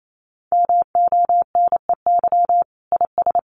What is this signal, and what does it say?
Morse code, heard as a keyed tone, about 28 wpm, 700 Hz. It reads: MONEY IS
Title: Sentences from Top 500 Words 28wpm